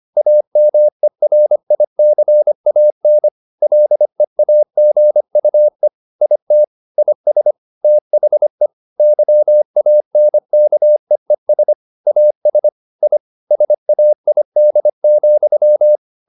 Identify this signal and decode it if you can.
Morse code, heard as a keyed tone, about 25 wpm, 600 Hz. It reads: AMERICAN LEAGUE IT IS THE YANKEES AS I SAID,
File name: the old man and the sea part 1 25wpm